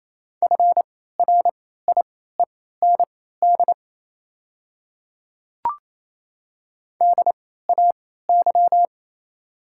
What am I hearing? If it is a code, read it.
Morse code, heard as a keyed tone, about 28 wpm, spaced as for 16 wpm, 700 Hz. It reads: FRIEND DAY
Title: 3x Character Spacing - Top 300 Words - 28wpm